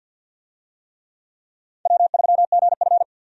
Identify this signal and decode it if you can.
Morse code, heard as a keyed tone, about 47 wpm, 700 Hz. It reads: W3GF
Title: USA Call Signs - 4 Characters - 50wpm